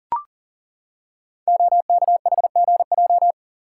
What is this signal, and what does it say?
Morse code, heard as a keyed tone, about 38 wpm, 690 Hz. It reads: OKHGJ
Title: Code Groups - 5 Letter:Numbers - 40wpm